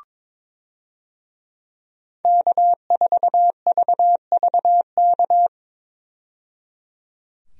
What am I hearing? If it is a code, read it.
Morse code, heard as a keyed tone, about 22 wpm, 700 Hz. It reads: K4VVK
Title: USA Call Signs - 5 Characters - 22wpm